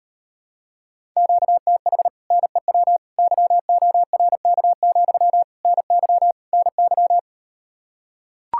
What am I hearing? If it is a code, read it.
Morse code, heard as a keyed tone, about 39 wpm, 700 Hz. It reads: QTH NEW YORK, NY NY